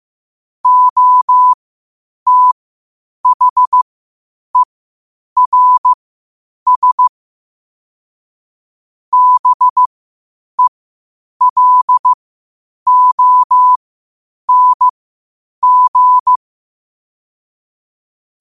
Morse code, heard as a keyed tone, about 15 wpm, spaced as for 8 wpm, 1000 Hz. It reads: OTHERS BELONG
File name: qotdfiles / 8wpm